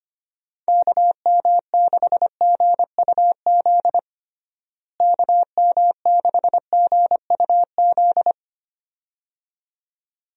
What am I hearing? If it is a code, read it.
Morse code, heard as a keyed tone, about 25 wpm, 700 Hz. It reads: KM6GUZ KM6GUZ
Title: USA Call Signs - 6 Characters - Repeated 2x - 25wpm